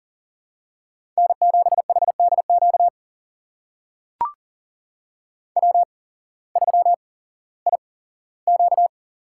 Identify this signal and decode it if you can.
Morse code, heard as a keyed tone, about 40 wpm, spaced as for 19 wpm, 700 Hz. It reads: N7HDQ W3IQ
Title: USA Call Signs - 8x Character Spacing - 40wpm